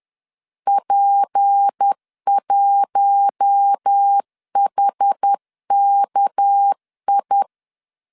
Morse code, heard as a keyed tone, about 10 wpm, 800 Hz. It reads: P1HKI
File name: LEZIONI / 6 PKX0 / 6 PK 10wpm 2436